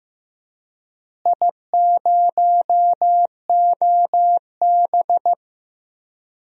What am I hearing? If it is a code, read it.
Morse code, heard as a keyed tone, about 15 wpm, 700 Hz. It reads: I0OB
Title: DXCC-100 Call Signs - 4 Characters - Repeated 3x - 15wpm